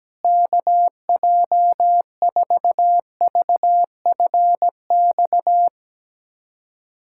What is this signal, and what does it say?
Morse code, heard as a keyed tone, about 17 wpm, 700 Hz. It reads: KJ4VFX
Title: USA Callsign DE Callsign - 6 Characters - 17wpm